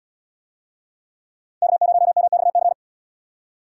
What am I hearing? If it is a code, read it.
Morse code, heard as a keyed tone, about 78 wpm, 700 Hz. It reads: K0MXC